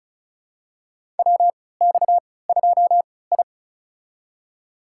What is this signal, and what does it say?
Morse code, heard as a keyed tone, about 36 wpm, spaced as for 20 wpm, 700 Hz. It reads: WX2I
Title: USA Call Signs - 3x Character Spacing - 35wpm